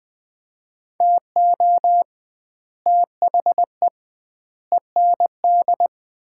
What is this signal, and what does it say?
Morse code, heard as a keyed tone, about 20 wpm, 700 Hz. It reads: TO THE END
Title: 2x Word-Spacing - Sets of 4 Words - 20wpm